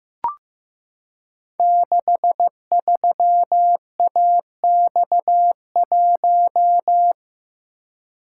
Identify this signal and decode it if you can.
Morse code, heard as a keyed tone, about 15 wpm, 700 Hz. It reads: 63AX1
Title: Code Groups - 5 Letter:Numbers - 15wpm